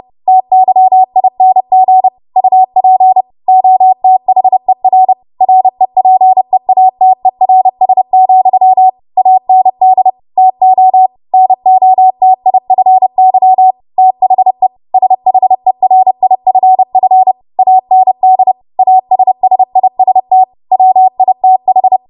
Morse code, heard as a keyed tone, about 30 wpm, 750 Hz. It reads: TYING UP OTHER REPEATERS, AND TO NOTIFY THE SHERIFF AND ASSIST WITH